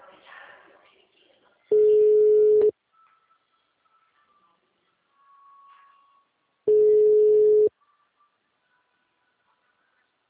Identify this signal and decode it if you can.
Ringback tone (Europe, 425 Hz).
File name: out- -2014- - - .51100